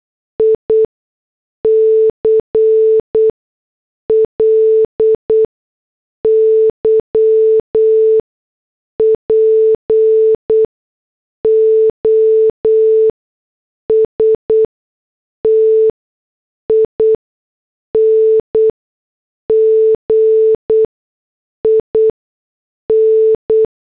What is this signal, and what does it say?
Morse code, heard as a keyed tone, about 8 wpm, 440 Hz. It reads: I C L Y P O S T I N G I N
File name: morse / headlines